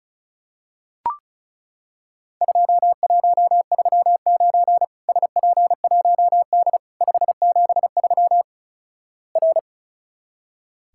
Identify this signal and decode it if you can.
Morse code, heard as a keyed tone, about 36 wpm, 700 Hz. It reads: 2139 SP1D 573 R
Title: Worked All Europe DX Contest - QTC Exchange - 35wpm